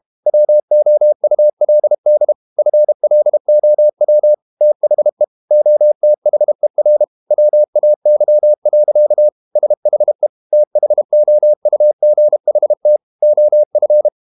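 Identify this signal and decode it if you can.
Morse code, heard as a keyed tone, about 32 wpm, 600 Hz. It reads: WOULD FLOW THE OTHER WAY. SHE THOUGHT OF
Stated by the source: Morse code practice at 32 wpm from CW med Gnister, reading Ken Follett's novel Never in English.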